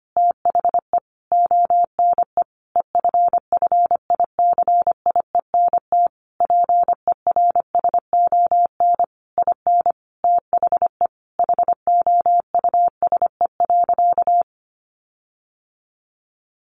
Morse code, heard as a keyed tone, about 25 wpm, 700 Hz. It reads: THE ONE EFFICIENT PERSON IN THE HOUSE.
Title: Sentences from Top 1300 Words 25wpm